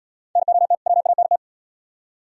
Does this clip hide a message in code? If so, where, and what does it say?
Yes: Morse code from 0.4 to 1.4 s, heard as a keyed tone, about 76 wpm, 700 Hz. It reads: NOT WANT